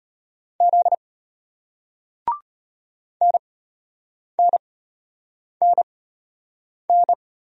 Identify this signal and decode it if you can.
Morse code, heard as a keyed tone, about 33 wpm, spaced as for 8 wpm, 700 Hz. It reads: Z NNNN